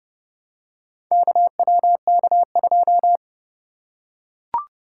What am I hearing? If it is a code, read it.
Morse code, heard as a keyed tone, about 26 wpm, 700 Hz. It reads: KWK2